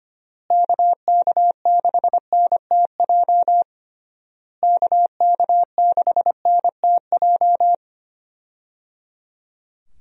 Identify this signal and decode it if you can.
Morse code, heard as a keyed tone, about 25 wpm, 700 Hz. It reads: KK6NTJ KK6NTJ
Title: USA Call Signs - 6 Characters - Repeated 3x - 25wpm